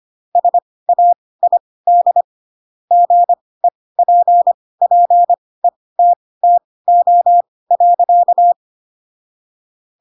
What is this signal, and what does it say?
Morse code, heard as a keyed tone, about 25 wpm, spaced as for 18 wpm, 700 Hz. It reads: SAID GEPPETTO.